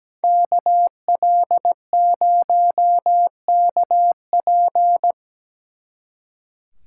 Morse code, heard as a keyed tone, about 20 wpm, 700 Hz. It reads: KL0KP
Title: Speed Racing - USA 2x2 Call Signs 17wpm